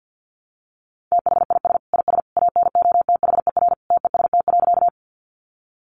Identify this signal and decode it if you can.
Morse code, heard as a keyed tone, about 50 wpm, 700 Hz. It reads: THIS IS ANOTHER TEST.
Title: Sentences from Top 600 Words 50wpm